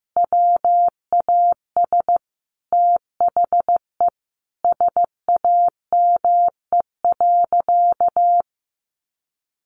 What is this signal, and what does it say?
Morse code, heard as a keyed tone, about 15 wpm, 700 Hz. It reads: WAS THE SAME.
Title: Sentences from Top 800 Words 15wpm